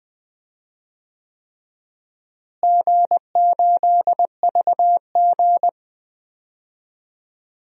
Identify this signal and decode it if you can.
Morse code, heard as a keyed tone, about 21 wpm, 700 Hz. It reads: G8VG
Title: DXCC-100 Call Signs - 4 Characters - Repeated 3x - 20wpm